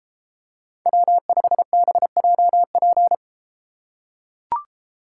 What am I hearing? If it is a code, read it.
Morse code, heard as a keyed tone, about 32 wpm, 700 Hz. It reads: W5BJP